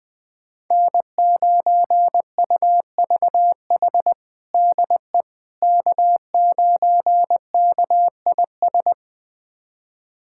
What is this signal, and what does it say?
Morse code, heard as a keyed tone, about 20 wpm, 700 Hz. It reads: N9UVH DE K9KIS
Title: USA Callsign DE Callsign - 5 Characters - 20wpm